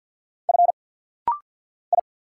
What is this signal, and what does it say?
Morse code, heard as a keyed tone, about 50 wpm, 700 Hz. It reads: F I
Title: Single Letters - Rapid-Fire 50wpm